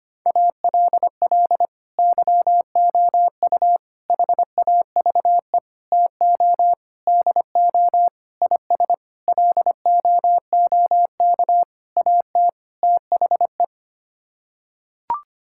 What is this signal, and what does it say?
Morse code, heard as a keyed tone, about 25 wpm, 700 Hz. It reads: ALL YOU HAVE TO DO IS LOOK AT THE
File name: Sets of 9 Words - 25wpm